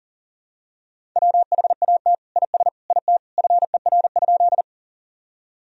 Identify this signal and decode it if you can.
Morse code, heard as a keyed tone, about 37 wpm, 700 Hz. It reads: WHAT IS IT FER?
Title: Sentences from Top 100 Words and Abbreviations 40wpm